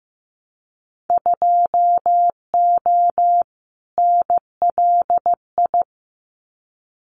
Morse code, heard as a keyed tone, about 15 wpm, 700 Hz. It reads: 2O NLI
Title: Winter Field Day Exchanges 15wpm